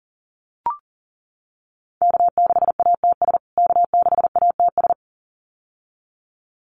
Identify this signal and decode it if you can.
Morse code, heard as a keyed tone, about 40 wpm, 700 Hz. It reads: K6ATS K6ATS
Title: USA Call Signs Repeated 2x 40wpm